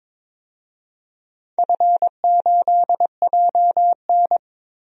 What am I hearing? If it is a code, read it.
Morse code, heard as a keyed tone, about 22 wpm, 700 Hz. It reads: F8JN
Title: DXCC-100 Call Signs - 4 Characters - Repeated 3x - 22wpm